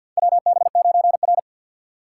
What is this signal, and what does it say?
Morse code, heard as a keyed tone, about 52 wpm, 700 Hz. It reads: WB9R
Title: USA Call Signs - 4 Characters - 3x Character Spacing - 50wpm